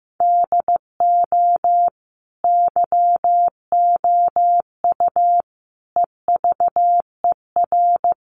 Morse code, heard as a keyed tone, about 15 wpm, 700 Hz. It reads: DO YOU EVER
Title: Sentences from Top 1500 Words 15wpm